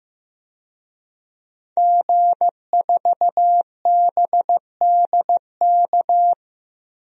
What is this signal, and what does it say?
Morse code, heard as a keyed tone, about 16 wpm, 700 Hz. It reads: G4BDK
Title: DXCC-100 Call Signs - 5 Characters - 15wpm